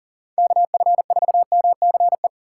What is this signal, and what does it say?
Morse code, heard as a keyed tone, about 48 wpm, 700 Hz. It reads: KF4MCE
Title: Speed Racing - USA 2x3 Call Signs 40wpm